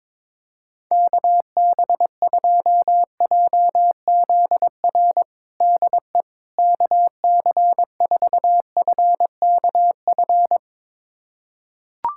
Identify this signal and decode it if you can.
Morse code, heard as a keyed tone, about 22 wpm, 700 Hz. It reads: KB2JZR DE KC4FKF
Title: USA Callsign DE Callsign - 6 Characters - 22wpm